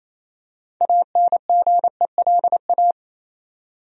Morse code, heard as a keyed tone, about 28 wpm, 700 Hz. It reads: ANGELA
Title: USA Common Names - 28wpm